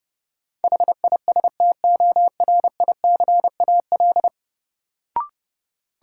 Morse code, heard as a keyed tone, about 32 wpm, 700 Hz. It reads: HISTORICAL
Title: Sentences from Top 600 Words - New Word Drill - 30wpm